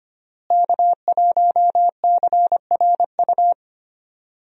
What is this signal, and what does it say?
Morse code, heard as a keyed tone, about 25 wpm, 700 Hz. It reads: K1CRU
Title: USA Call Signs - 5 Characters - Repeated 3x - 25wpm